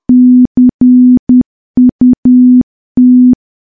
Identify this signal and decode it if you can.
Morse code, heard as a keyed tone, about 10 wpm, 250 Hz. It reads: CUT
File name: morse / morse-training-10WPM